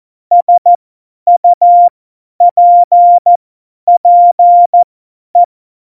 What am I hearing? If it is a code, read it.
Morse code, heard as a keyed tone, about 14 wpm, spaced as for 10 wpm, 700 Hz. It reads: SUPPE
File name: Morsekurs / Witze / Witz 25. Juni 2019, 50 F70